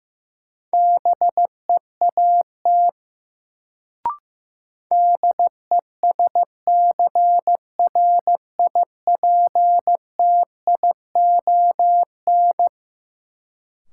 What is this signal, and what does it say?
Morse code, heard as a keyed tone, about 15 wpm, 700 Hz. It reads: BEAT DESCRIPTION
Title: Sentences from Top 1300 Words - New Word Drill - 15wpm